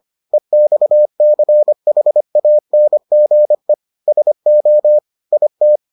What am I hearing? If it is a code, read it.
Morse code, heard as a keyed tone, about 25 wpm, 600 Hz. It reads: EXCHANGE SO IT